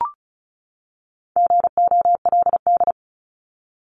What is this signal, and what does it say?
Morse code, heard as a keyed tone, about 34 wpm, 700 Hz. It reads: GOLD
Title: Top 500 Words 35wpm